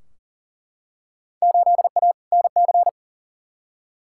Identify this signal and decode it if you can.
Morse code, heard as a keyed tone, about 40 wpm, 700 Hz. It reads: 8A NC
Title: Field Day Exchanges - 40wpm